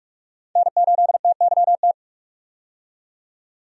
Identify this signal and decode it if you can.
Morse code, heard as a keyed tone, about 46 wpm, 700 Hz. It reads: N8TYT
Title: USA Call Signs - 5 Characters - 45wpm